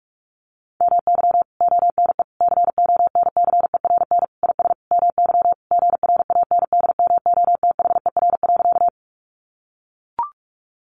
Morse code, heard as a keyed tone, about 44 wpm, 700 Hz. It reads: MY ONE CONCERN IS MY GRANDMOTHER.
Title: Sentences from Top 1200 Words 45wpm